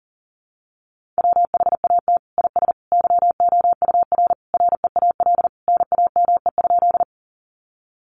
Morse code, heard as a keyed tone, about 37 wpm, 700 Hz. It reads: WHAT IS YOUR REAL NAME?
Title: Sentences from Top 300 Words 40wpm